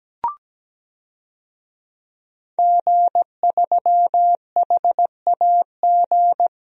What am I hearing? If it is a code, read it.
Morse code, heard as a keyed tone, about 17 wpm, 700 Hz. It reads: G3HAG
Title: DXCC-100 Call Signs - 5 Characters - 17wpm